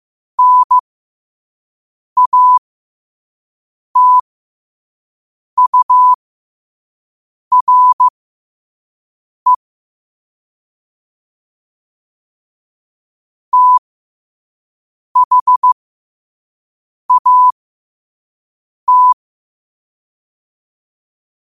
Morse code, heard as a keyed tone, about 15 wpm, spaced as for 5 wpm, 1000 Hz. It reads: NATURE THAT
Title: 5 WPM morse code quotes for Fri, 01 Mar 2024